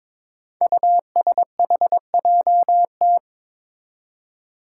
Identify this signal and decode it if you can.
Morse code, heard as a keyed tone, about 21 wpm, 700 Hz. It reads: USHJT
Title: Five Challenging Letter-Number Combinations 22wpm